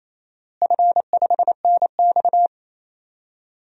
Morse code, heard as a keyed tone, about 28 wpm, 700 Hz. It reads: F5NX
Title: DXCC-100 Call Signs - 4 Characters - Repeated 3x - 28wpm